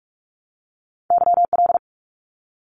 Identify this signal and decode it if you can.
Morse code, heard as a keyed tone, about 44 wpm, 700 Hz. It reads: YL
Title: Abbreviations Repeated 3x - 45wpm